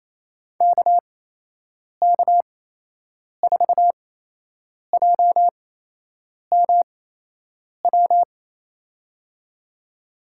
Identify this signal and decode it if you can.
Morse code, heard as a keyed tone, about 28 wpm, spaced as for 8 wpm, 700 Hz. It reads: KK4JMW
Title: USA Call Signs - 6 Characters - 8x Character Spacing - 28wpm